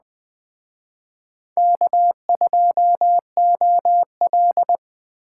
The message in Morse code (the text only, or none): K2OL